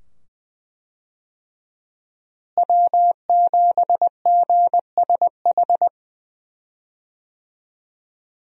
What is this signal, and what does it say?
Morse code, heard as a keyed tone, about 20 wpm, 700 Hz. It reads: W7GSH
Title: USA Call Signs - 5 Characters - Repeated 2x - 20wpm